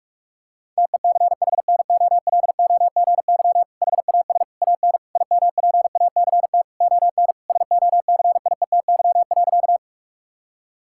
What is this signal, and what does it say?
Morse code, heard as a keyed tone, about 41 wpm, 700 Hz. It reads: TECHNOLOGY HAS AN IMPACT ON SOCIETY.
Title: Sentences from Top 1000 Words - Review Part 6 - 45wpm